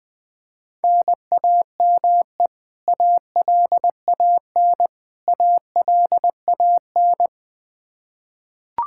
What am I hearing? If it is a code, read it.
Morse code, heard as a keyed tone, about 20 wpm, 700 Hz. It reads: NAME ALAN ALAN
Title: Practice QSO - Exchange 1 - Beginning - 20wpm